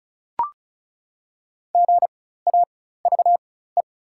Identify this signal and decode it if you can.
Morse code, heard as a keyed tone, about 36 wpm, spaced as for 16 wpm, 690 Hz. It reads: GAVE
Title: 4x Character Spacing - Top 500 Words - 35wpm